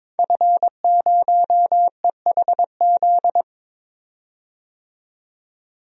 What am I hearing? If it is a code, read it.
Morse code, heard as a keyed tone, about 21 wpm, 700 Hz. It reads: F0EHZ
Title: DXCC-100 Call Signs - 5 Characters - 22wpm